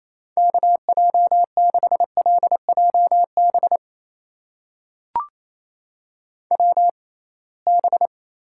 Morse code, heard as a keyed tone, about 28 wpm, spaced as for 21 wpm, 700 Hz. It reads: KJ6LJB WB
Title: USA Call Signs - 6 Characters - 6x Character Spacing - 28wpm